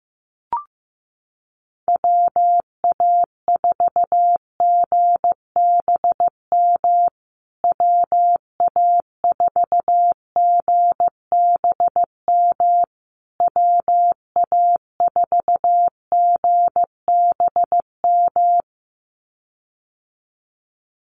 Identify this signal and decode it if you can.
Morse code, heard as a keyed tone, about 15 wpm, 700 Hz. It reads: WA4GBM WA4GBM WA4GBM
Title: USA Call Signs Repeated 3x 15wpm